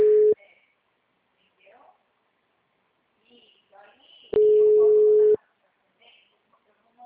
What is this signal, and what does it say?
Ringback tone (Europe, 425 Hz), with speech.
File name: out- -2030- - - .11253